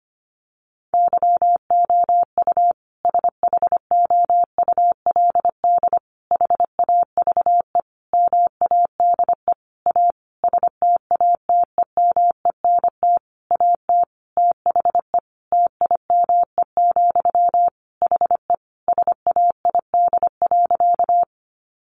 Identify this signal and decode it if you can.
Morse code, heard as a keyed tone, about 25 wpm, 700 Hz. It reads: YOU SHOULD HAVE MADE A STATEMENT AT THE TIME, HE SAID.
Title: Sentences from Top 1200 Words 25wpm